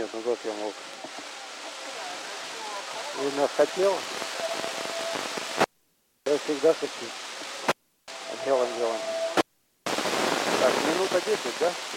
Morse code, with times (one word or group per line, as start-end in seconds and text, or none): none